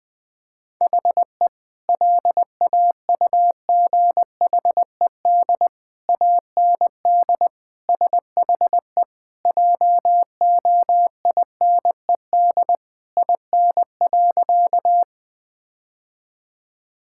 Morse code, heard as a keyed tone, about 20 wpm, 700 Hz. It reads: HE LAUGHED AND SHE JOINED IN.
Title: Sentences with 2-Word Phrases - Spoken Beginning - 20wpm